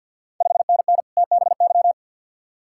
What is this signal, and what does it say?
Morse code, heard as a keyed tone, about 48 wpm, 690 Hz. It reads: 5NN TBY
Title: Ontario QSO Party Exchange 50wpm